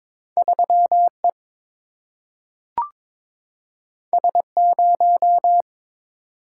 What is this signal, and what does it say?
Morse code, heard as a keyed tone, about 22 wpm, 700 Hz. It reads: 3 E S 0
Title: Single-Letter Single-Number 22wpm